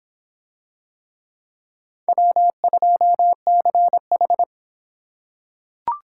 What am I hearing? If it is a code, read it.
Morse code, heard as a keyed tone, about 24 wpm, 700 Hz. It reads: W2CH